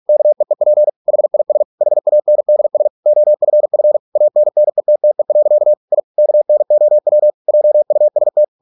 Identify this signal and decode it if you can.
Morse code, heard as a keyed tone, about 46 wpm, 600 Hz. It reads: KEEP HIS HANDS OFF ANNETTE. I KNOW JUST